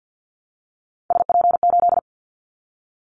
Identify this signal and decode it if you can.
Morse code, heard as a keyed tone, about 50 wpm, 700 Hz. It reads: SP8
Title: Callsign Prefixes for Top 25 DXCC - 50wpm